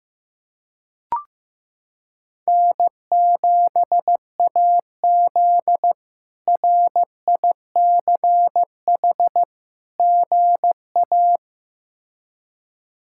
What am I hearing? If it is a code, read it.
Morse code, heard as a keyed tone, about 16 wpm, 700 Hz. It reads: N7AZ RICH GA
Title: K1USN SST QSO Practice - 15wpm